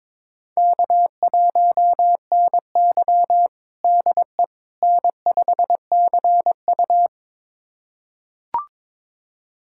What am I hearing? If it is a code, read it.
Morse code, heard as a keyed tone, about 22 wpm, 700 Hz. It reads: K1NY DE N5CU
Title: USA Callsign DE Callsign - 4 Characters - 22wpm